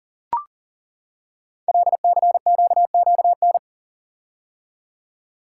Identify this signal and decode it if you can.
Morse code, heard as a keyed tone, about 38 wpm, 700 Hz. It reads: LCQQN
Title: Code Groups - 5 Letters - 40wpm